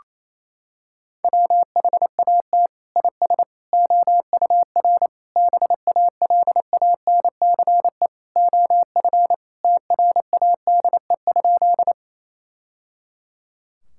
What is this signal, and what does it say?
Morse code, heard as a keyed tone, about 28 wpm, 700 Hz. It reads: WHAT IS OUR BALANCE OF TRADE?
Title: Sentences from Top 1000 Words - Review Part 6 - 28wpm